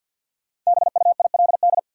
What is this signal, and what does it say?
Morse code, heard as a keyed tone, about 52 wpm, 700 Hz. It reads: BUILD
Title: Sentences from Top 1100 Words - New Word Drill Alt - 50wpm